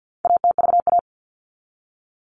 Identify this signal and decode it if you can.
Morse code, heard as a keyed tone, about 50 wpm, 700 Hz. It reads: ATVA
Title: All Cut Numbers - 4-Digit 50wpm